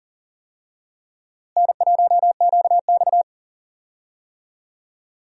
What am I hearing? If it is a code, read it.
Morse code, heard as a keyed tone, about 40 wpm, 700 Hz. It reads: N1QX
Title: USA Call Signs - 4 Characters - Repeated 3x - 40wpm